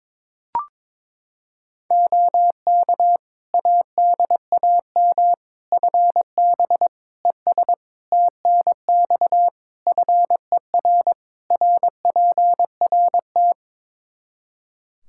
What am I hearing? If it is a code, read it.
Morse code, heard as a keyed tone, about 22 wpm, 700 Hz. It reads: OK ADAM FB ES TNX FER RPRT
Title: Practice QSO - Exchange 2 - Info - 22wpm